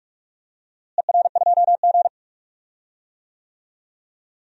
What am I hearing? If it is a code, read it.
Morse code, heard as a keyed tone, about 42 wpm, 690 Hz. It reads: ER2G
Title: DXCC-100 Call Signs - 4 Characters - Repeated 3x - 45wpm